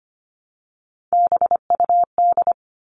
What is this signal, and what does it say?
Morse code, heard as a keyed tone, about 25 wpm, 700 Hz. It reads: BUD
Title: All Cut Numbers - 3-Digit 25wpm